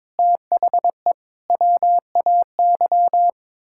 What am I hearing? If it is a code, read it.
Morse code, heard as a keyed tone, about 21 wpm, 700 Hz. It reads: THE WAY
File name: practice / 22wpm / Sets of 7 Words - 22wpm